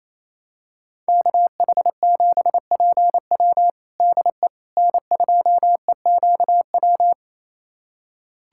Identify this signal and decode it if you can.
Morse code, heard as a keyed tone, about 27 wpm, 700 Hz. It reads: KH7PW DE N2EQW
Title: USA Callsign DE Callsign - 5 Characters - 28wpm